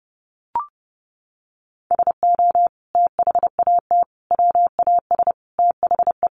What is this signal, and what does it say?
Morse code, heard as a keyed tone, about 30 wpm, 700 Hz. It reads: SO THAT WAS THE
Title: Sentences from Top 100 Words 30wpm